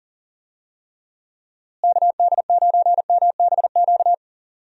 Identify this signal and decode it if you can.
Morse code, heard as a keyed tone, about 40 wpm, 700 Hz. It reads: KD9MBQ
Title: USA Call Signs - 6 Characters - Repeated 2x - 40wpm